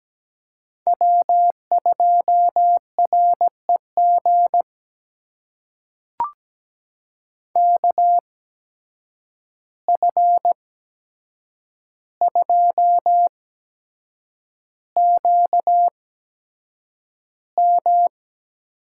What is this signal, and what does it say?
Morse code, heard as a keyed tone, about 17 wpm, spaced as for 7 wpm, 700 Hz. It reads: W2REG KF2QM
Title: USA Call Signs - 5 Characters - 8x Character Spacing - 17wpm